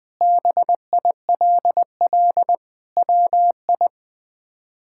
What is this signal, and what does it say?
Morse code, heard as a keyed tone, about 20 wpm, 700 Hz. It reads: BILL WI
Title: K1USN SST QSO Practice - 20wpm